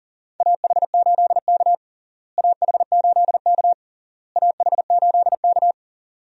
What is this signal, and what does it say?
Morse code, heard as a keyed tone, about 40 wpm, 700 Hz. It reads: AH8K AH8K AH8K
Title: USA Call Signs - 4 Characters - Repeated 3x - 40wpm